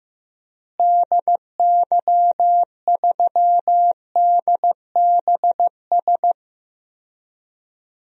Morse code, heard as a keyed tone, about 15 wpm, 700 Hz. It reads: DY3DBS
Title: DXCC-100 Call Signs - 6 Characters - Repeated 2x - 15wpm